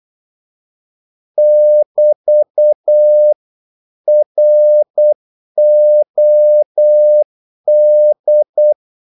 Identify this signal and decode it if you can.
Morse code, heard as a keyed tone, about 8 wpm, 600 Hz. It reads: <BT> ROD
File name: Korte ord DK 8wpm